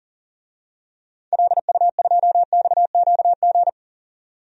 Morse code, heard as a keyed tone, about 40 wpm, 700 Hz. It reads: LU2XQG
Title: DXCC-100 Call Signs - 6 Characters - 40wpm